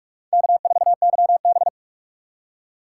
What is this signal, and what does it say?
Morse code, heard as a keyed tone, about 46 wpm, 690 Hz. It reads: K4YB
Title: USA Call Signs - 4 Characters - 8x Character Spacing - 45wpm